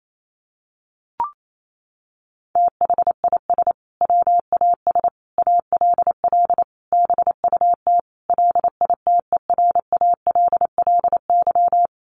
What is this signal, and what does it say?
Morse code, heard as a keyed tone, about 28 wpm, 700 Hz. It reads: THIS WAS ALL BUT LITERALLY
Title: Sentences from Top 1400 Words 28wpm